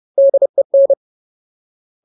Morse code, heard as a keyed tone, about 30 wpm, 550 Hz. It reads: DEN
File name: Ultimate Word List / 30WPM Ultimate